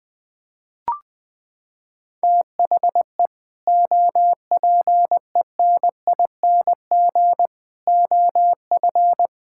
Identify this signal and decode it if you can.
Morse code, heard as a keyed tone, about 20 wpm, 700 Hz. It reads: THE OPENING OF
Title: Sentences with 3-Word Phrases - 20wpm